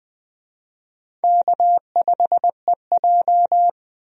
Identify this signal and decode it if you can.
Morse code, heard as a keyed tone, about 20 wpm, 690 Hz. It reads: K5EJ
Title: USA Call Signs - 4 Characters - Repeated 3x - 20wpm